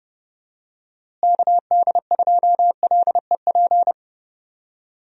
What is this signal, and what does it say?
Morse code, heard as a keyed tone, about 30 wpm, 700 Hz. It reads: KD2LEP
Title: USA Call Signs - 6 Characters - Repeated 3x - 30wpm